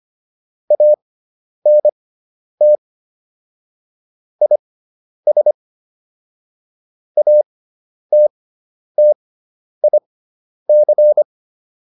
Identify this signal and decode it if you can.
Morse code, heard as a keyed tone, about 25 wpm, spaced as for 10 wpm, 600 Hz. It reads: ANT IS ATTIC
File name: QSO210 10